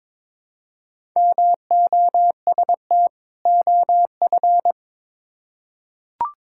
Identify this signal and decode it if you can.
Morse code, heard as a keyed tone, about 22 wpm, 700 Hz. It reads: MOST OF
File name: Build A Phrase - 2-Word Phrases - 22wpm